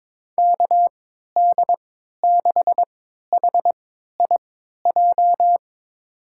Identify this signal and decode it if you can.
Morse code, heard as a keyed tone, about 22 wpm, spaced as for 12 wpm, 700 Hz. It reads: KD6HIJ
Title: USA Call Signs - 3x Character Spacing - 22wpm